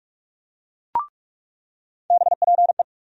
Morse code, heard as a keyed tone, about 43 wpm, 700 Hz. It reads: BPE
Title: USA Call Sign Postfixes - 3 Letters 45wpm